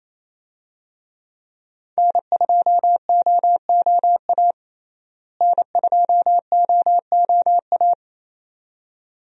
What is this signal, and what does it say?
Morse code, heard as a keyed tone, about 27 wpm, 700 Hz. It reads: N2OOA N2OOA
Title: USA Call Signs - 5 Characters - Repeated 2x - 28wpm